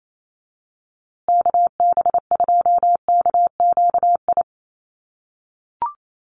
Morse code, heard as a keyed tone, about 28 wpm, 700 Hz. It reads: KB2KQI
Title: USA Call Signs Repeated 3x 28wpm